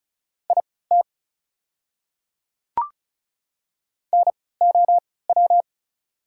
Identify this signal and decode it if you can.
Morse code, heard as a keyed tone, about 36 wpm, spaced as for 20 wpm, 700 Hz. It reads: IT NOW
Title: 3x Character Spacing - Top 200 Words - 35wpm